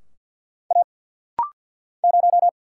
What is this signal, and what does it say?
Morse code, heard as a keyed tone, about 50 wpm, 700 Hz. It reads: A 0
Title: Single Letter-Number - Rapid-Fire 50wpm